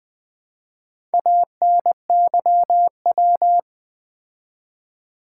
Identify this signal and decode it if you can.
Morse code, heard as a keyed tone, about 20 wpm, 700 Hz. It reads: ANYW
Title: Build A Word - 4 Letters At A Time - 8-Letter Words - 20wpm